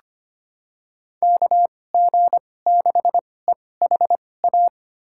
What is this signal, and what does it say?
Morse code, heard as a keyed tone, about 25 wpm, spaced as for 18 wpm, 700 Hz. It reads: KG6EHA
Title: USA Call Signs - 6 Characters - 2x Character Spacing - 25wpm